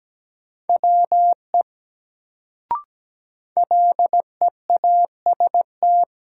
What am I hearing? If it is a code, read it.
Morse code, heard as a keyed tone, about 17 wpm, 700 Hz. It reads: WE LEAST
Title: Sets of 2 Words - New Word Drill - 17wpm